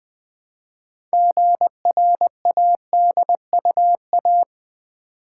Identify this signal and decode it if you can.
Morse code, heard as a keyed tone, about 20 wpm, 690 Hz. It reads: GRADUA
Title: Build A Word - 3 Letters At A Time - 8-Letter Words - 20wpm